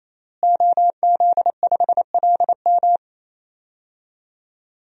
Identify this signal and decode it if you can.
Morse code, heard as a keyed tone, about 28 wpm, 700 Hz. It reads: OZ5LM
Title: DXCC-100 Call Signs - 5 Characters - Repeated 3x - 28wpm